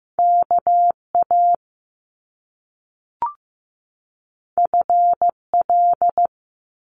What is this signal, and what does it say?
Morse code, heard as a keyed tone, about 15 wpm, 700 Hz. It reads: KA FL
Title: Top Two Letter Combinations in Words 15wpm